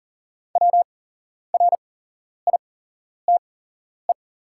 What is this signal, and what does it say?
Morse code, heard as a keyed tone, about 38 wpm, spaced as for 11 wpm, 700 Hz. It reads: WRITE
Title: 8x Character Spacing - Top 200 Words - 40wpm